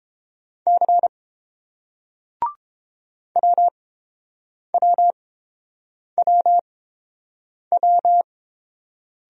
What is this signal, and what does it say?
Morse code, heard as a keyed tone, about 29 wpm, spaced as for 7 wpm, 700 Hz. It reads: C WWWW